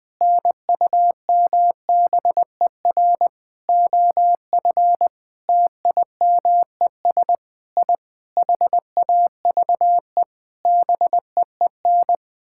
Morse code, heard as a keyed tone, about 20 wpm, 700 Hz. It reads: NUMBER OF TIMES I HAVE BEEN
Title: Sentences with 9-Word Phrases - 20wpm